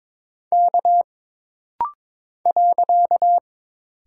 Morse code, heard as a keyed tone, about 22 wpm, 700 Hz. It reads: K .
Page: This rapid-fire timing is geared to push your ICR (Instant Character Recognition) skills.